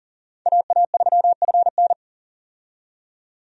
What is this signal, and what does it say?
Morse code, heard as a keyed tone, about 40 wpm, 700 Hz. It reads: AA3FN
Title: USA Call Signs - 5 Characters - 40wpm